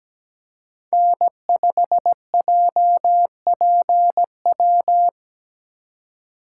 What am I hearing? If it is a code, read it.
Morse code, heard as a keyed tone, about 17 wpm, 700 Hz. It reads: N5JPW
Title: USA Call Signs - 3x Character Spacing - 17wpm